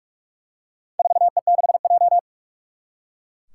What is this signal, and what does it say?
Morse code, heard as a keyed tone, about 43 wpm, 700 Hz. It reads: 4E6J
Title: Four Challenging Letter-Number Combinations 45wpm